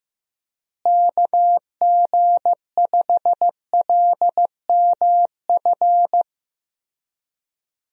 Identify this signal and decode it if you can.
Morse code, heard as a keyed tone, about 15 wpm, 700 Hz. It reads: KG5LMF
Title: USA Call Signs - 6 Characters - Repeated 2x - 15wpm